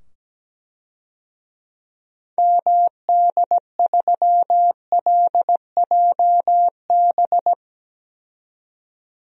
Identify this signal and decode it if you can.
Morse code, heard as a keyed tone, about 17 wpm, 700 Hz. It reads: MD3LJB
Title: DXCC-100 Call Signs - 6 Characters - 17wpm